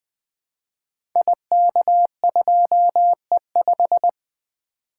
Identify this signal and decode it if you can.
Morse code, heard as a keyed tone, about 20 wpm, 700 Hz. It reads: IK2E5
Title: Code Groups - 5 Letter:Numbers - 20wpm